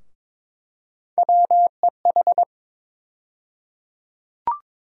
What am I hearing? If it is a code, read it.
Morse code, heard as a keyed tone, about 22 wpm, 700 Hz. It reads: WEH
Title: USA Call Sign Postfixes - 3 Letters 22wpm